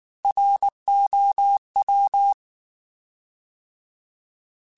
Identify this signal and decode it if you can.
Morse code, heard as a keyed tone, about 19 wpm, 800 Hz. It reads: ROW
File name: Alphabetical / Alphabetical mp3s / Rs @18